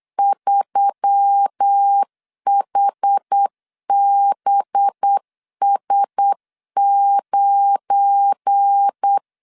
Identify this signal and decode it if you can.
Morse code, heard as a keyed tone, about 8 wpm, 800 Hz. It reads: 3HBS9
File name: LEZIONI / 7 GF9Z / 7 GF9 8wpm 3545